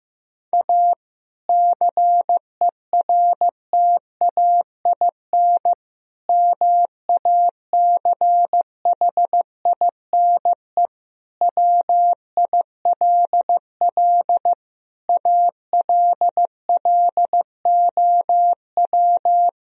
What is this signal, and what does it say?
Morse code, heard as a keyed tone, about 16 wpm, 700 Hz. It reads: A CERTAIN MACHINE WILL ALLOW
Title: Sentences from Top 1000 Words - Review Part 2 - 15wpm